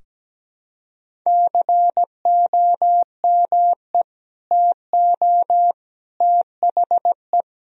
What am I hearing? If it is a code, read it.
Morse code, heard as a keyed tone, about 17 wpm, 700 Hz. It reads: COME TO THE
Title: Sentences from Top 1000 Words - Review Part 1 - 17wpm